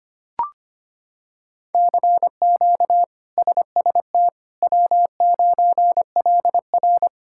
Morse code, heard as a keyed tone, about 25 wpm, 700 Hz. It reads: CQ SST W9LR
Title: K1USN SST QSO Practice - 25wpm